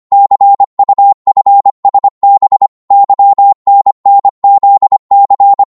Morse code, heard as a keyed tone, about 25 wpm, 800 Hz. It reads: CUFSB YNNZC
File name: Morse / ltr-25wpm